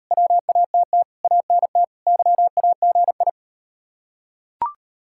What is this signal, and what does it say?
Morse code, heard as a keyed tone, about 35 wpm, 690 Hz. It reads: WATT ANT YAGI